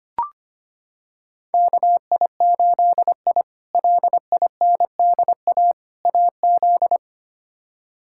Morse code, heard as a keyed tone, about 25 wpm, 700 Hz. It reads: KI8I LINDA AZ
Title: K1USN SST QSO Practice - 25wpm